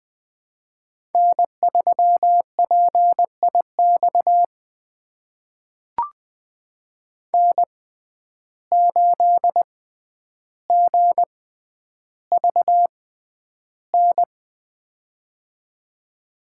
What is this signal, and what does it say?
Morse code, heard as a keyed tone, about 20 wpm, spaced as for 10 wpm, 700 Hz. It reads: N3PIX N8GVN
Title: USA Call Signs - 5 Characters - 6x Character Spacing - 20wpm